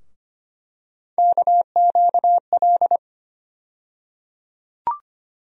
Morse code, heard as a keyed tone, about 25 wpm, 700 Hz. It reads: KQL
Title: USA Call Sign Postfixes - 3 Letters 25wpm